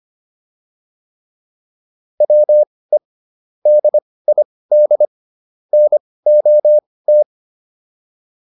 Morse code, heard as a keyed tone, about 25 wpm, spaced as for 18 wpm, 600 Hz. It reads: WE DID NOT